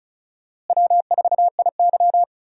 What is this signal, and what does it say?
Morse code, heard as a keyed tone, about 34 wpm, 700 Hz. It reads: W4IY
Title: USA Call Signs - 4 Characters - Repeated 3x - 35wpm